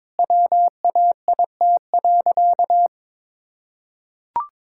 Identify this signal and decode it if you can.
Morse code, heard as a keyed tone, about 22 wpm, 700 Hz. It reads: WAIT.
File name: Sentences with 9-Word Phrases - 22wpm